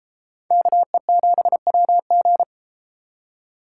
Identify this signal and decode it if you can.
Morse code, heard as a keyed tone, about 32 wpm, 700 Hz. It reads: KE7WG